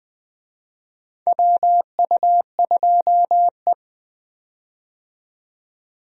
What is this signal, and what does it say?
Morse code, heard as a keyed tone, about 20 wpm, 700 Hz. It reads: WU2E
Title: USA Call Signs - 4 Characters - 20wpm